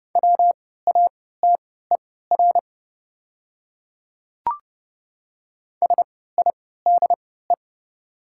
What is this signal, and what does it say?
Morse code, heard as a keyed tone, about 30 wpm, spaced as for 17 wpm, 700 Hz. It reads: WATER SIDE
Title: 3x Character Spacing - Top 100 Words - 30wpm